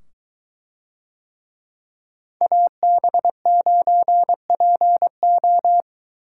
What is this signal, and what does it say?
Morse code, heard as a keyed tone, about 23 wpm, 700 Hz. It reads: AB9PO